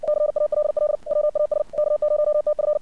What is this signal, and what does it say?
Morse code, heard as a keyed tone, about 56 wpm, 600 Hz. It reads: CALL WAS K0TF
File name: archived / CRSnet